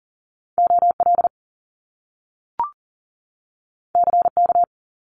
Morse code, heard as a keyed tone, about 40 wpm, 700 Hz. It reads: OL CK
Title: Top Two Letter Combinations in Words 40wpm